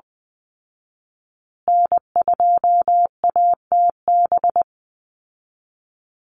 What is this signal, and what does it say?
Morse code, heard as a keyed tone, about 20 wpm, 700 Hz. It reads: N2ATB
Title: Call Signs Repeated 3x - 20wpm